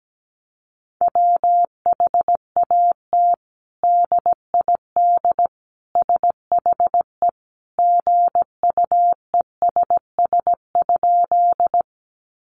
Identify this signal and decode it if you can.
Morse code, heard as a keyed tone, about 17 wpm, 700 Hz. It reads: WHAT DID SHE GUESS?
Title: Sentences from Top 1500 Words 17wpm